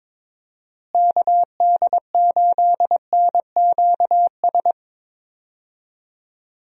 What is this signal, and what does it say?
Morse code, heard as a keyed tone, about 22 wpm, 700 Hz. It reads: KD8NQS
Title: USA Call Signs - 6 Characters - Repeated 3x - 22wpm